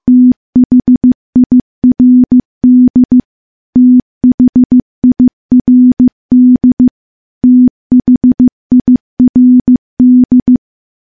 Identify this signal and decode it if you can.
Morse code, heard as a keyed tone, about 15 wpm, 250 Hz. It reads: THIRD THIRD THIRD
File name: morse / morse-training-15WPM